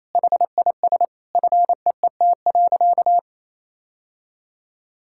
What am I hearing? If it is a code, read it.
Morse code, heard as a keyed tone, about 28 wpm, 700 Hz. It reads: HIS FEET.
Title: Sentences from Top 1000 Words - Review Part 5 - 28wpm